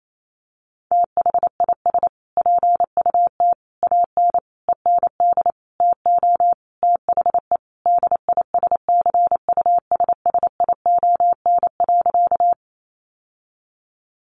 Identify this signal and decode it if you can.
Morse code, heard as a keyed tone, about 28 wpm, 700 Hz. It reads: THIS PUT AN END TO THE DISCUSSION.
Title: Sentences from Top 1100 Words 28wpm